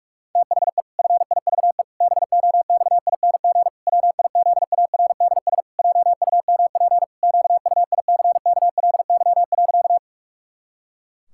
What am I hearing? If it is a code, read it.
Morse code, heard as a keyed tone, about 41 wpm, 700 Hz. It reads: THE FIVE BOXING WIZARDS JUMP QUICKLY.
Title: Pangram Challenge v2 - 45wpm